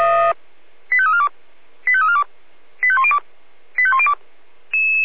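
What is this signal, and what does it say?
Five-tone selective call (ZVEI), heard as quick runs of five tones, tones of about 80 ms. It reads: 85342 85342 85102 85102
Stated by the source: ZVEI five-tone call received on an SDR.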